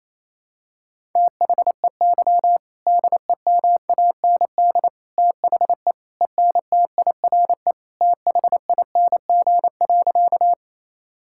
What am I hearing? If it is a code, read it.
Morse code, heard as a keyed tone, about 27 wpm, 700 Hz. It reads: THEY DEMAND THE ENTIRE THING.
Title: Sentences from Top 1000 Words - Review Part 5 - 28wpm